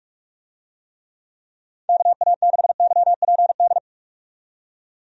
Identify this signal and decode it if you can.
Morse code, heard as a keyed tone, about 42 wpm, 700 Hz. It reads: KA6YPD
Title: USA Call Signs - 6 Characters - 45wpm